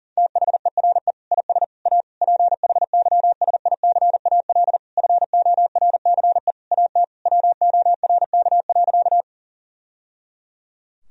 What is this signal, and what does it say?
Morse code, heard as a keyed tone, about 40 wpm, 700 Hz. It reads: THERE IS A PHYSICAL FORCE AT WORK.
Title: Sentences from Top 1000 Words - Review Part 4 - 40wpm